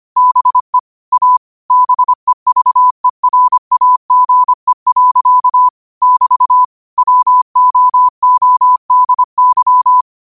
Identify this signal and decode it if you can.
Morse code, heard as a keyed tone, about 25 wpm, 1000 Hz. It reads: BE A BEVERAGE. <BT> WOODY
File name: qotdfiles / 25wpm